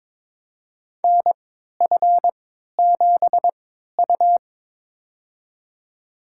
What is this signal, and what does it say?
Morse code, heard as a keyed tone, about 21 wpm, spaced as for 12 wpm, 700 Hz. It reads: NF7U
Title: USA Call Signs - 4 Characters - 3x Character Spacing - 22wpm